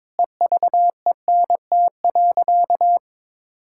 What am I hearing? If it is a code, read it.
Morse code, heard as a keyed tone, about 22 wpm, 700 Hz. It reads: EVENT.
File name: Sentences with 7-Word Phrases - 22wpm